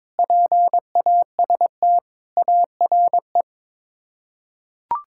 Morse code, heard as a keyed tone, about 23 wpm, 700 Hz. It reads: PAST ARE
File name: practice / 22wpm / Sets of 8 Words - 22wpm